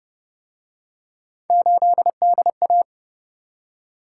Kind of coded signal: Morse code

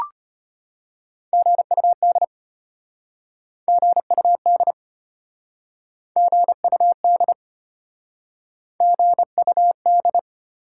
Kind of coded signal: Morse code

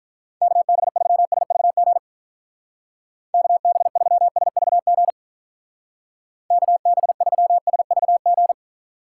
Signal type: Morse code

